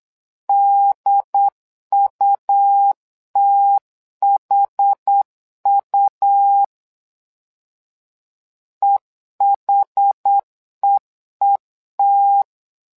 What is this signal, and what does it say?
Morse code, heard as a keyed tone, about 8 wpm, 800 Hz. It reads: DUTHU EHEET
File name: LEZIONI / 2 UHDN / 2 UHD 8wpm 2258